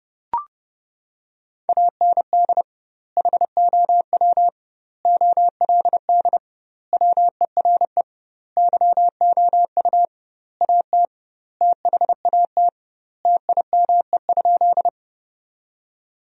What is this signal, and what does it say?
Morse code, heard as a keyed tone, about 30 wpm, 700 Hz. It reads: AND HOW OLD WERE YOU AT THAT TIME?
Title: 2x Word-Spacing - Sentences from Top 200 Words - 30wpm